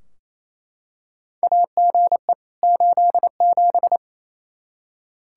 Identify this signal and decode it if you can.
Morse code, heard as a keyed tone, about 28 wpm, 700 Hz. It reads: AGE 87
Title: Practice QSO - Exchange 3 - More Info - 28wpm